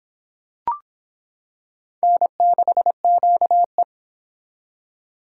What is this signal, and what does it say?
Morse code, heard as a keyed tone, about 26 wpm, 700 Hz. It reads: N6QE